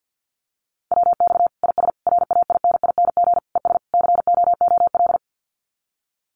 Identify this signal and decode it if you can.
Morse code, heard as a keyed tone, about 50 wpm, 700 Hz. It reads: WX IS RAINING ES COOL
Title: Common QSO Phrases 50wpm